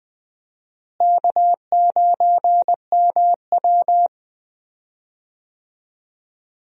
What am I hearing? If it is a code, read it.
Morse code, heard as a keyed tone, about 20 wpm, 700 Hz. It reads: K9MW
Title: USA Call Signs - 4 Characters - Repeated 2x - 20wpm